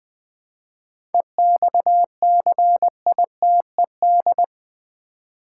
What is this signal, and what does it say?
Morse code, heard as a keyed tone, about 20 wpm, 700 Hz. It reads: EXCITED
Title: 7 Letter Words - First 3 Letters Given - 20wpm